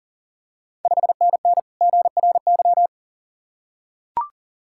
Morse code, heard as a keyed tone, about 40 wpm, 690 Hz. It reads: 5NN GRY
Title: Ontario QSO Party Exchange 40wpm